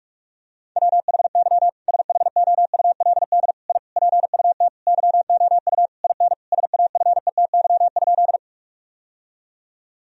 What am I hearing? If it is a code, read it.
Morse code, heard as a keyed tone, about 46 wpm, 700 Hz. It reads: WHY SHOULD I PUT YOU IN SAFETY?
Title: Sentences from Top 1000 Words - Review Part 4 - 45wpm